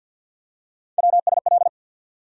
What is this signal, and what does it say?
Morse code, heard as a keyed tone, about 50 wpm, 700 Hz. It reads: WSL
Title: USA Call Sign Postfixes - 3 Letters 50wpm